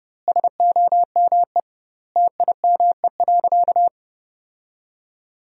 Morse code, heard as a keyed tone, about 30 wpm, 700 Hz. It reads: SOME TIME.
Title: 2x Word-Spacing - Sentences from Top 300 Words - 30wpm